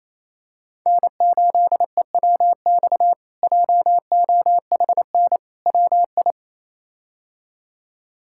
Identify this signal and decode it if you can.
Morse code, heard as a keyed tone, about 28 wpm, 700 Hz. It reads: N8EWX JOHN WI
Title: K1USN SST QSO Practice - 28wpm